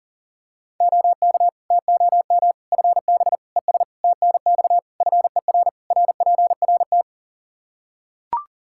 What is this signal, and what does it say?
Morse code, heard as a keyed tone, about 37 wpm, 700 Hz. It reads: OK TOM FB ES TNX FER RPRT
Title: Practice QSO - Exchange 2 - Info - 40wpm